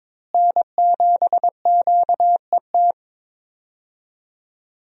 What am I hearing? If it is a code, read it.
Morse code, heard as a keyed tone, about 22 wpm, 700 Hz. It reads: N7QET
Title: USA Call Signs - 5 Characters - Repeated 2x - 22wpm